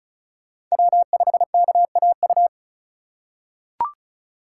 Morse code, heard as a keyed tone, about 36 wpm, 700 Hz. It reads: W5KAU
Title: USA Call Signs - 5 Characters - 3x Character Spacing - 35wpm